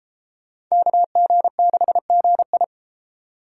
Morse code, heard as a keyed tone, about 33 wpm, 700 Hz. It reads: KG6GI